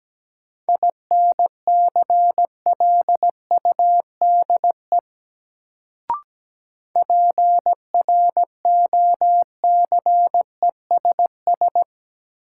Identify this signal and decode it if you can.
Morse code, heard as a keyed tone, about 17 wpm, 700 Hz. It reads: INCLUDE PROCESS
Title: Sentences from Top 1000 Words - Review Part 3 - New Word Drill - 17wpm